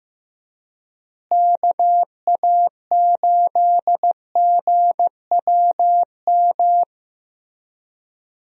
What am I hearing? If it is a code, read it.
Morse code, heard as a keyed tone, about 15 wpm, 700 Hz. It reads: KA8GWM
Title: USA Call Signs - 6 Characters - Repeated 2x - 15wpm